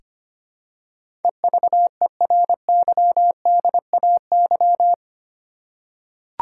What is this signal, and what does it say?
Morse code, heard as a keyed tone, about 25 wpm, 700 Hz. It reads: EVERYDAY
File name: Eight Letter Words - 25wpm